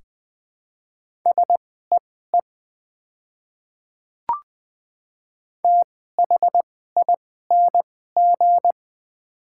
Morse code, heard as a keyed tone, about 21 wpm, spaced as for 15 wpm, 700 Hz. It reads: SEE THING
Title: 2x Character Spacing - Top 100 Words - 20wpm